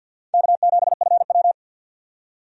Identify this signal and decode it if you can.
Morse code, heard as a keyed tone, about 52 wpm, 700 Hz. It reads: K7FW
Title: USA Call Signs - 4 Characters - 50wpm